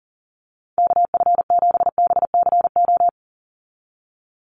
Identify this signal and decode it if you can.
Morse code, heard as a keyed tone, about 40 wpm, 700 Hz. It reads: KF7BCO
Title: USA Call Signs 40wpm